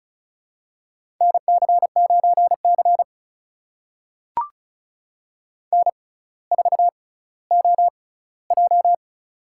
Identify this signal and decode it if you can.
Morse code, heard as a keyed tone, about 36 wpm, spaced as for 18 wpm, 700 Hz. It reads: NC9C N4OJ
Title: USA Call Signs - 4 Characters - 6x Character Spacing - 35wpm